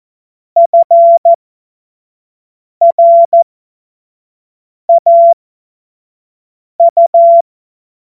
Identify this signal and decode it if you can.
Morse code, heard as a keyed tone, about 14 wpm, spaced as for 5 wpm, 680 Hz. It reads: FRAU
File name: Morsekurs / Witze / Witz 14. März 2023 25 F70